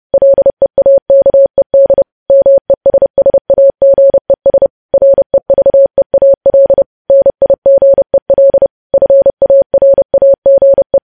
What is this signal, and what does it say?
Morse code, heard as a keyed tone, about 30 wpm, 550 Hz. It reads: LEAKED MESSAGES REVEAL NIGEL FARAGE
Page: news headlines in morse code at 30 wpm